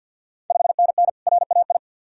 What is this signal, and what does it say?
Morse code, heard as a keyed tone, about 50 wpm, 690 Hz. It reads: 5NN RAI
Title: Ontario QSO Party Exchange 50wpm